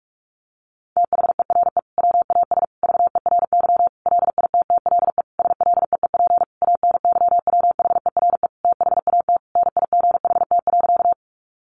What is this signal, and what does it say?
Morse code, heard as a keyed tone, about 44 wpm, 700 Hz. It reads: THERE WAS VERY LITTLE SLEEP ANYWHERE THAT NIGHT.
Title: Sentences from Top 1400 Words 45wpm